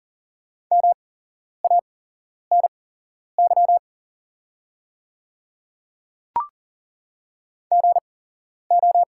Morse code, heard as a keyed tone, about 40 wpm, spaced as for 11 wpm, 690 Hz. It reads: MANY GO
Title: 8x Character Spacing - Top 100 Words - 40wpm